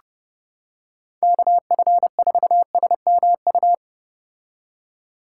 Morse code, heard as a keyed tone, about 31 wpm, 700 Hz. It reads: KF4SMU